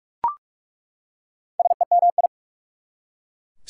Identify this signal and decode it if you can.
Morse code, heard as a keyed tone, about 45 wpm, 690 Hz. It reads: SEMI
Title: Prefixes and Suffixes 45wpm